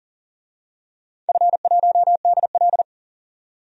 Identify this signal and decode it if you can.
Morse code, heard as a keyed tone, about 39 wpm, 700 Hz. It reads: F1DL
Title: DXCC-100 Call Signs - 4 Characters - 40wpm